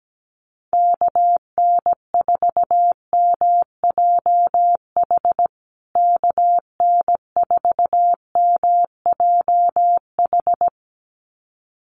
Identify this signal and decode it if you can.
Morse code, heard as a keyed tone, about 17 wpm, 700 Hz. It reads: KN4MJH KN4MJH
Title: USA Call Signs Repeated 2x 17wpm